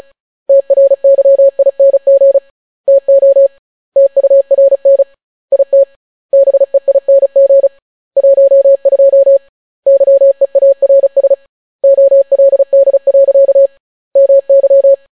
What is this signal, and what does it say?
Morse code, heard as a keyed tone, about 36 wpm, 550 Hz. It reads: TRYING TO TURN IT BEING 12 YEARS OLD. MY